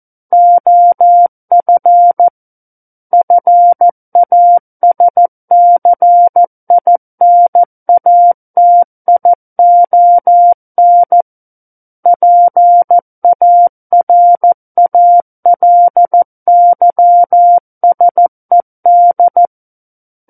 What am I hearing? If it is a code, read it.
Morse code, heard as a keyed tone, about 14 wpm, 700 Hz. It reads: OF FASCINATION PARALYSED
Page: War of the Worlds - 05-Chapter 5 - 14 WPM